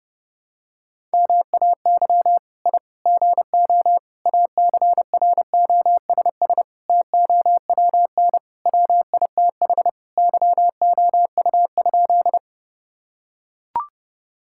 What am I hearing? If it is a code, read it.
Morse code, heard as a keyed tone, about 30 wpm, 700 Hz. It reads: MAY I GO ACROSS TOWN WITH YOU?
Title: Sentences from Top 1000 Words - Review Part 5 - 30wpm